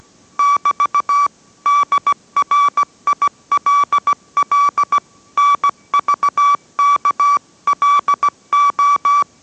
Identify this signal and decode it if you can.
Morse code, heard as a keyed tone, about 17 wpm, 1150 Hz. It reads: <BT> DRILL NVKLO